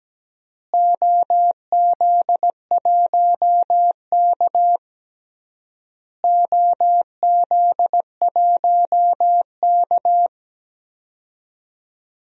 Morse code, heard as a keyed tone, about 17 wpm, 700 Hz. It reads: OZ1K OZ1K
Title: DXCC-100 Call Signs - 4 Characters - Repeated 3x - 17wpm